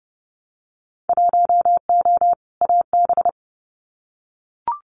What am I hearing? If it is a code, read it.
Morse code, heard as a keyed tone, about 30 wpm, 700 Hz. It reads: 1O AB
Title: Winter Field Day Exchanges 30wpm